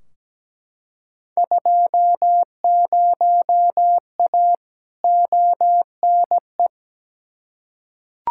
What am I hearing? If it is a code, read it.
Morse code, heard as a keyed tone, about 17 wpm, 700 Hz. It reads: 20A ONE
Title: Field Day Exchanges Encore - 17wpm